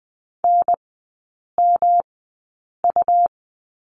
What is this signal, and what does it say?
Morse code, heard as a keyed tone, about 20 wpm, spaced as for 8 wpm, 700 Hz. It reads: NMU
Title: 107 Lesson 18e - Letter m - Three Characters 20wpm